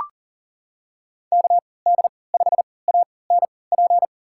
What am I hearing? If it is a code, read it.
Morse code, heard as a keyed tone, about 38 wpm, spaced as for 22 wpm, 700 Hz. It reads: KD5ANP
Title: USA Call Signs - 6 Characters - 3x Character Spacing - 40wpm